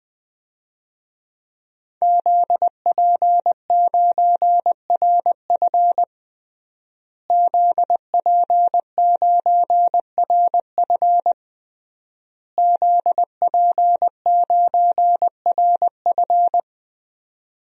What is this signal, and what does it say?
Morse code, heard as a keyed tone, about 21 wpm, 700 Hz. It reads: ZP9RF ZP9RF ZP9RF
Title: DXCC-100 Call Signs - 5 Characters - Repeated 3x - 20wpm